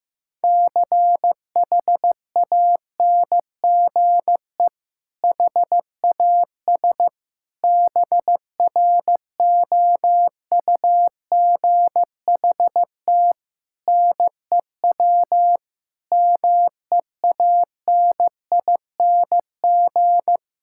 Morse code, heard as a keyed tone, about 15 wpm, 700 Hz. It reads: CHANGE HAS BROUGHT NEW MEANING
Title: Sentences from Top 1600 Words 15wpm